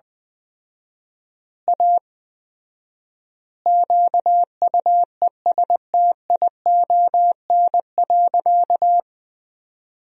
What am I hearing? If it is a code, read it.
Morse code, heard as a keyed tone, about 20 wpm, 700 Hz. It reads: A QUESTION.
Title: 4x Word-Spacing - Sentences from Top 400 Words - 20wpm